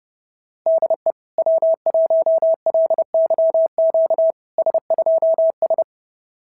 Morse code, heard as a keyed tone, about 28 wpm, 690 Hz. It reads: DE W1LYQ S2S
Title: SOTA QSO Practice 30wpm